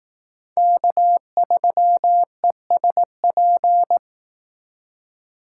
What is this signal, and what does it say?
Morse code, heard as a keyed tone, about 18 wpm, 700 Hz. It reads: K3ESP
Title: Speed Racing - USA 1x3 Call Signs 15wpm